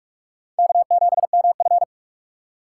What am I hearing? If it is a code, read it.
Morse code, heard as a keyed tone, about 41 wpm, 700 Hz. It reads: K7MF